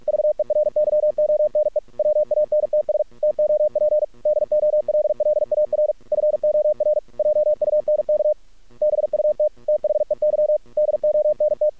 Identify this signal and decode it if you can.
Morse code, heard as a keyed tone, about 46 wpm, 600 Hz. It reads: VERYONE WANTS TOP DOLLAR FOR JUNK BUT THEY DONT
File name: archived / CRSnet